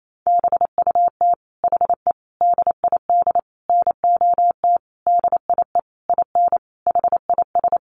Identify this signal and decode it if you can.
Morse code, heard as a keyed tone, about 28 wpm, 700 Hz. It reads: BUT HE DID NOT DIE IN HIS
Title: Sentences from Top 1300 Words 28wpm